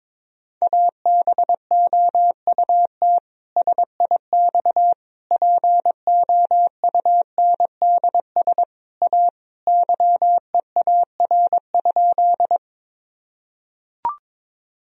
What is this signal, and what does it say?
Morse code, heard as a keyed tone, about 22 wpm, 700 Hz. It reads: ABOUT SIX POUNDS A YEAR?
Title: Sentences from Top 1000 Words - Review Part 2 - 22wpm